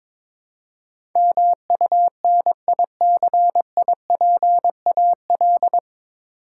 Morse code, heard as a keyed tone, about 22 wpm, 700 Hz. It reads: MUNICIPAL
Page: Morse Code Ninja practice 22wpm